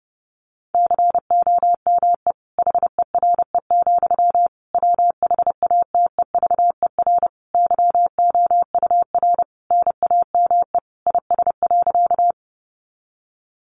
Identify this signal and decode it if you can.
Morse code, heard as a keyed tone, about 28 wpm, 700 Hz. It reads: COME HERE, WHATEVER YOUR NAME IS.
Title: Sentences from Top 1100 Words 30wpm